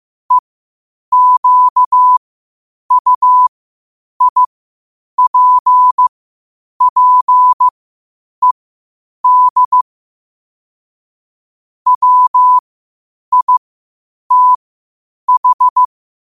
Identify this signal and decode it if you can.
Morse code, heard as a keyed tone, about 15 wpm, spaced as for 8 wpm, 1000 Hz. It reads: EQUIPPED WITH
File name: qotdfiles / 8wpm